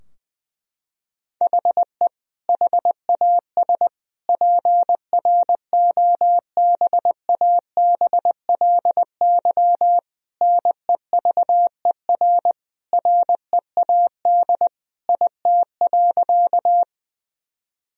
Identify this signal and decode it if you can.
Morse code, heard as a keyed tone, about 20 wpm, 700 Hz. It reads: HE HAS PROBABLY NEVER READ IT.
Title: Sentences from Top 1000 Words - Review Part 4 - 20wpm